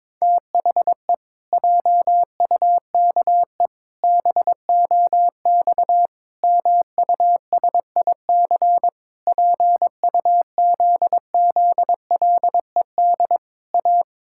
Morse code, heard as a keyed tone, about 22 wpm, 700 Hz. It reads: THE JUKE BOX MUSIC PUZZLED A
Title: Pangram Challenge v2 - 22wpm